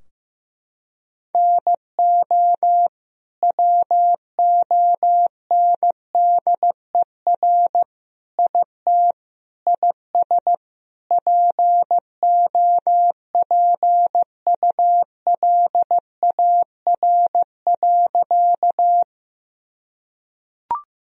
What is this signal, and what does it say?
Morse code, heard as a keyed tone, about 16 wpm, 700 Hz. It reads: NO WONDER IT IS POPULAR.
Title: Sentences from Top 1000 Words - Review Part 3 - 15wpm